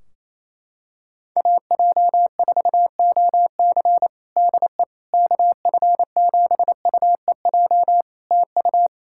Morse code, heard as a keyed tone, about 28 wpm, 700 Hz. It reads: AJ4OC DE KF7UEJ TU
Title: Practice QSO - Exchange 4 - Ending - 28wpm